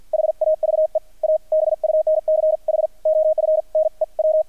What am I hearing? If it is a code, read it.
Morse code, heard as a keyed tone, about 48 wpm, 630 Hz. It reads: HAVE A BUNCH OF NEW
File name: archived / CRSnet